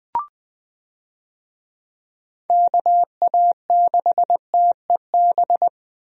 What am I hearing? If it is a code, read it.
Morse code, heard as a keyed tone, about 20 wpm, 700 Hz. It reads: KA6TEB
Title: USA Call Signs - 6 Characters - Repeated 3x - 20wpm